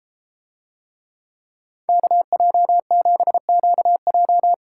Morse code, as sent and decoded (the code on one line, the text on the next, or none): -.- .--- --... --.- .---
KJ7QJ